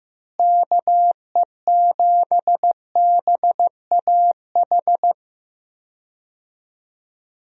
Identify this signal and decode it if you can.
Morse code, heard as a keyed tone, about 15 wpm, 700 Hz. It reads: KE7BAH
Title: USA Call Signs - 6 Characters - Repeated 2x - 15wpm